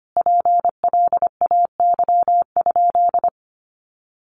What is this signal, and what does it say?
Morse code, heard as a keyed tone, about 25 wpm, 700 Hz. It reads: PLAY?
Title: Sentences from Top 300 Words 25wpm